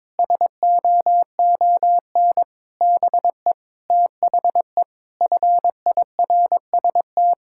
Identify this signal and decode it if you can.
Morse code, heard as a keyed tone, about 22 wpm, 700 Hz. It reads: SOON BE THE FIRST
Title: Sentences from Top 1000 Words - Review Part 5 - 22wpm